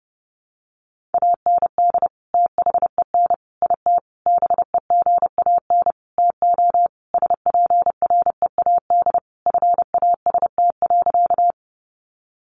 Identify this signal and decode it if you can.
Morse code, heard as a keyed tone, about 28 wpm, 700 Hz. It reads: AND THEN IT BEGAN TO SPREAD FAST.
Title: Sentences from Top 1200 Words 30wpm